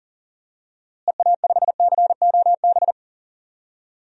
Morse code, heard as a keyed tone, about 37 wpm, 700 Hz. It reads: EA5COB
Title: DXCC-100 Call Signs - 6 Characters - Repeated 3x - 40wpm